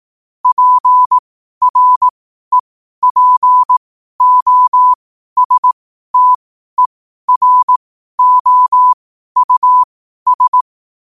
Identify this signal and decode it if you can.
Morse code, heard as a keyed tone, about 18 wpm, spaced as for 12 wpm, 1000 Hz. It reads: PREPOSTEROUS
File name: qotdfiles / 12wpm